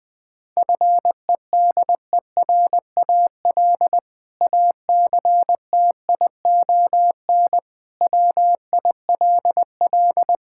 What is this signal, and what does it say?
Morse code, heard as a keyed tone, about 21 wpm, 700 Hz. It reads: FEDERAL ACTION WILL
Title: Sentences from Top 1000 Words - Review Part 5 - 20wpm